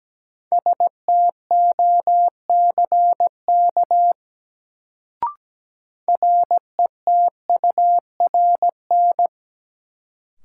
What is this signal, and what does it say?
Morse code, heard as a keyed tone, about 17 wpm, 700 Hz. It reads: STOCK RETURN
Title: Sentences from Top 800 Words - New Word Drill - 17wpm